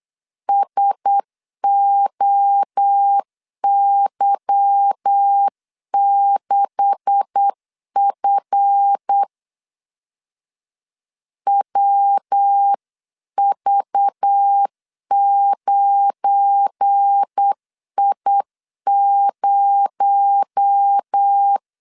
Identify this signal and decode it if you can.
Morse code, heard as a keyed tone, about 8 wpm, 800 Hz. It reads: SOY6F WV9I0
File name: LEZIONI / 9 286 8wpm 3425